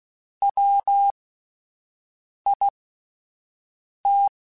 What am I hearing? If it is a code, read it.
Morse code, heard as a keyed tone, about 16 wpm, spaced as for 5 wpm, 800 Hz. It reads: WIT
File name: Spaced Characters / Spaced Characters-mp3s / Weather4 21 sp @10